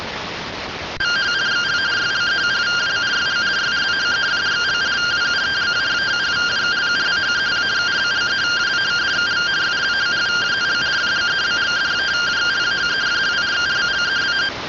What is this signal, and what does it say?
Two-tone teleprinter signal, about 170 Hz shift.